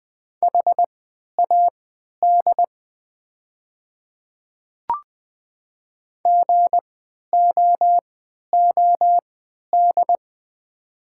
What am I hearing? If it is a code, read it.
Morse code, heard as a keyed tone, about 20 wpm, spaced as for 11 wpm, 690 Hz. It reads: HAD GOOD
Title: 3x Character Spacing - Top 200 Words - 20wpm